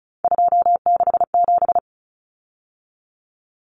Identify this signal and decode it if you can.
Morse code, heard as a keyed tone, about 34 wpm, 700 Hz. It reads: 267
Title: Numbers - 3-Digit 35wpm